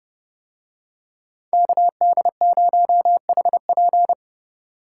Morse code, heard as a keyed tone, about 30 wpm, 700 Hz. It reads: KD0HP